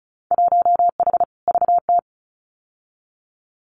Morse code, heard as a keyed tone, about 34 wpm, 700 Hz. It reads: 1H VT
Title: Winter Field Day Exchanges Encore 35wpm